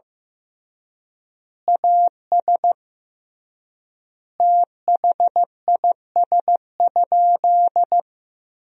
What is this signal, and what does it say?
Morse code, heard as a keyed tone, about 15 wpm, 700 Hz. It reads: AS THIS?
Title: 3x Word-Spacing - Sentences from Top 400 Words - 15wpm